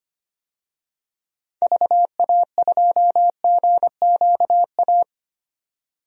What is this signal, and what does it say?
Morse code, heard as a keyed tone, about 25 wpm, 700 Hz. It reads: VA2GQA
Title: DXCC-100 Call Signs - 6 Characters - 25wpm